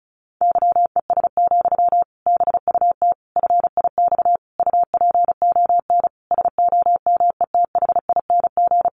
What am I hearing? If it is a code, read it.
Morse code, heard as a keyed tone, about 34 wpm, 700 Hz. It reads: YES, BUT FIX UPON SOMETHING
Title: Sentences from Top 1400 Words 35wpm